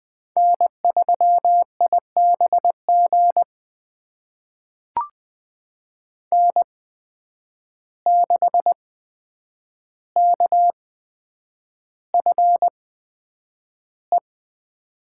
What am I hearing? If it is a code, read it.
Morse code, heard as a keyed tone, about 20 wpm, spaced as for 9 wpm, 700 Hz. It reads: N3IBG N6KFE
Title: USA Call Signs - 5 Characters - 8x Character Spacing - 20wpm